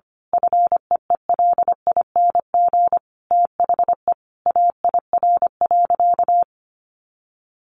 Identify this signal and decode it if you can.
Morse code, heard as a keyed tone, about 25 wpm, 700 Hz. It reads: FEELING THE AIR.
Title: A Sound Of Thunder 25wpm